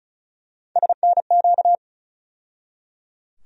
Morse code, heard as a keyed tone, about 36 wpm, 700 Hz. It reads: SNQ
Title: USA Call Sign Postfixes - 3 Letters 35wpm